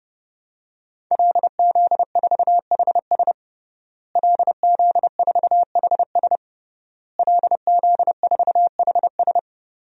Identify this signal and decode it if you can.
Morse code, heard as a keyed tone, about 31 wpm, 700 Hz. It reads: LZ4HS LZ4HS LZ4HS
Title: DXCC-100 Call Signs - 5 Characters - Repeated 3x - 30wpm